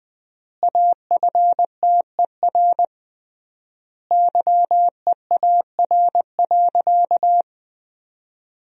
Morse code, heard as a keyed tone, about 20 wpm, 700 Hz. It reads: AFTER YEAR.
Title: 3x Word-Spacing - Sentences from Top 300 Words - 20wpm